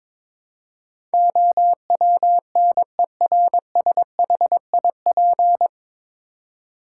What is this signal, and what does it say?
Morse code, heard as a keyed tone, about 22 wpm, 700 Hz. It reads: OWNERSHIP
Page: Morse Code Ninja practice 22wpm